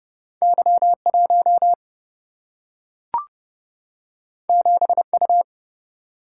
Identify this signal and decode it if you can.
Morse code, heard as a keyed tone, about 32 wpm, 700 Hz. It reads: Y 1 7 U
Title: Single-Letter Single-Number 30wpm